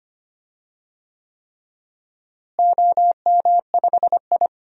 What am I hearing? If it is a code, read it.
Morse code, heard as a keyed tone, about 25 wpm, 700 Hz. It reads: OM5I
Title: DXCC-100 Call Signs - 4 Characters - Repeated 3x - 25wpm